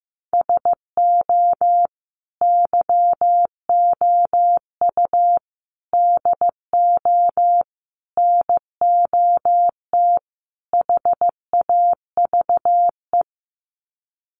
Morse code, heard as a keyed tone, about 15 wpm, 700 Hz. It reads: SO YOU DO NOT HAVE
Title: Sets of 5 Words Encore 15wpm